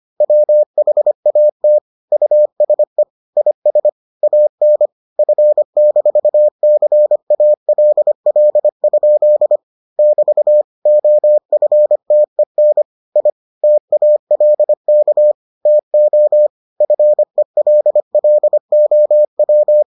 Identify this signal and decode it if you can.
Morse code, heard as a keyed tone, about 25 wpm, 600 Hz. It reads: WHAT USE IS AN F-CALL? <BT> OFTEN I TALK TO FELLOW